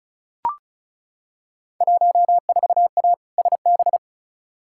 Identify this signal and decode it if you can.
Morse code, heard as a keyed tone, about 36 wpm, 690 Hz. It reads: 14A SB
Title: Field Day Exchanges Encore - 35wpm